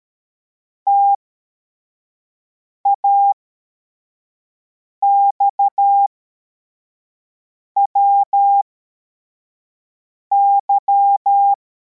Morse code, heard as a keyed tone, about 13 wpm, spaced as for 4 wpm, 800 Hz. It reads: TAXWY
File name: formation / cw / Text73